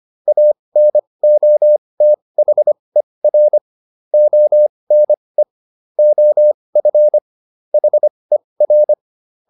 Morse code, heard as a keyed tone, about 25 wpm, spaced as for 20 wpm, 600 Hz. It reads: ANOTHER ONE OF HER